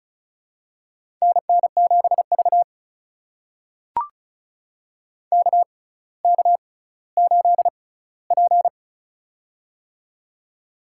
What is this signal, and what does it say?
Morse code, heard as a keyed tone, about 36 wpm, spaced as for 18 wpm, 700 Hz. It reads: NN7V KK8P
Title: USA Call Signs - 6x Character Spacing - 35wpm